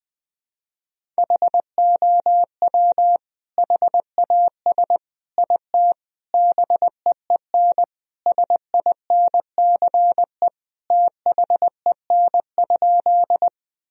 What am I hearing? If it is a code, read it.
Morse code, heard as a keyed tone, about 20 wpm, 700 Hz. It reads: HOW HAS IT BEEN SINCE THEN?
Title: Sentences from Top 1000 Words - Review Part 1 - 20wpm